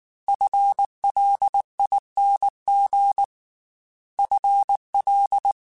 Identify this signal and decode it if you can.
Morse code, heard as a keyed tone, about 19 wpm, 800 Hz. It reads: FLING FL
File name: FL / FL mp3s / FL Prefixes 1x1 @18